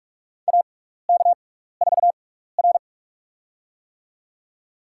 Morse code, heard as a keyed tone, about 46 wpm, spaced as for 15 wpm, 700 Hz. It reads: AK4R
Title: USA Call Signs - 4 Characters - 6x Character Spacing - 45wpm